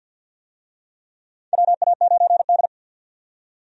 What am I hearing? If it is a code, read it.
Morse code, heard as a keyed tone, about 48 wpm, 700 Hz. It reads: WA9D
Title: USA Call Signs - 4 Characters - 50wpm